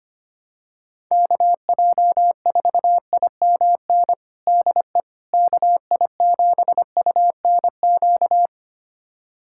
Morse code, heard as a keyed tone, about 25 wpm, 700 Hz. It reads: KJ4IMN DE KI7UNQ
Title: USA Callsign DE Callsign - 6 Characters - 25wpm